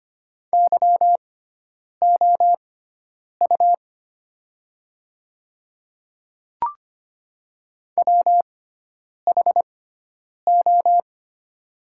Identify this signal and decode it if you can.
Morse code, heard as a keyed tone, about 25 wpm, spaced as for 9 wpm, 690 Hz. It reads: YOU WHO
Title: 6x Character Spacing - Top 100 Words - 25wpm